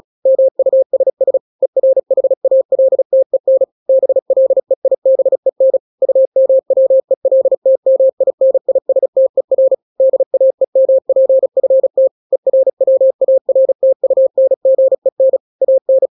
Morse code, heard as a keyed tone, about 36 wpm, 530 Hz. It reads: MUSS ERHALTEN BLEIBEN UMWELTMINISTER DAEMPFT ERWARTUNGEN AN
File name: daily / 35wpm